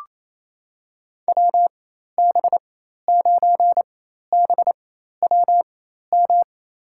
Morse code, heard as a keyed tone, about 27 wpm, spaced as for 13 wpm, 700 Hz. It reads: WB9BWM
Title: USA Call Signs - 4x Character Spacing - 28wpm